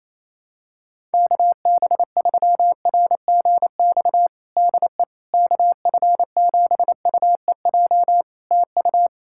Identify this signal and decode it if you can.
Morse code, heard as a keyed tone, about 28 wpm, 700 Hz. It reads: KB3RGX DE KF7UEJ TU
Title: Practice QSO - Exchange 4 - Ending - 28wpm